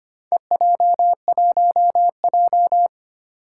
Morse code, heard as a keyed tone, about 25 wpm, 700 Hz. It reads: EJ1J
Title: Four Challenging Letter-Number Combinations 25wpm